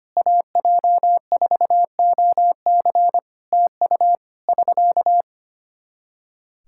Morse code, heard as a keyed tone, about 25 wpm, 700 Hz. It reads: AJ4OC TU <SK>
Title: Practice QSO - Exchange 4 - Ending - 25wpm